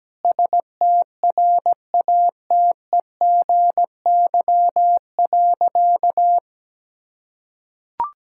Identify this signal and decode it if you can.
Morse code, heard as a keyed tone, about 17 wpm, 700 Hz. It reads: STRATEGY.
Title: Sentences from Top 1000 Words - Review Part 6 - 17wpm